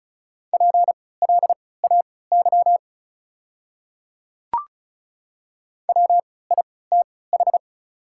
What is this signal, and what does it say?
Morse code, heard as a keyed tone, about 34 wpm, spaced as for 20 wpm, 690 Hz. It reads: PLAY WITH
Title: 3x Character Spacing - Top 200 Words - 35wpm